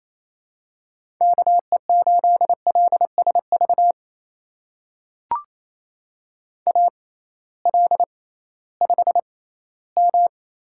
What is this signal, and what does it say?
Morse code, heard as a keyed tone, about 28 wpm, spaced as for 16 wpm, 700 Hz. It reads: KE8LSV AL5M
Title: USA Call Signs - 6x Character Spacing - 28wpm